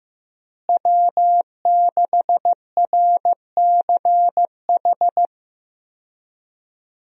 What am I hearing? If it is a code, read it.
Morse code, heard as a keyed tone, about 15 wpm, 700 Hz. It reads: W6RCH
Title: USA Call Signs - 5 Characters - Repeated 2x - 15wpm